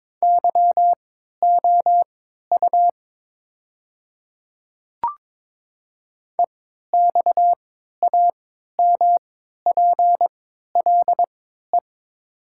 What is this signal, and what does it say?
Morse code, heard as a keyed tone, about 22 wpm, spaced as for 12 wpm, 700 Hz. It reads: YOU EXAMPLE
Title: 3x Character Spacing - Top 300 Words - 22wpm